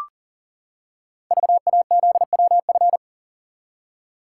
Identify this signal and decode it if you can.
Morse code, heard as a keyed tone, about 39 wpm, 700 Hz. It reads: VAZWF
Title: Code Groups - 5 Letter:Numbers - 40wpm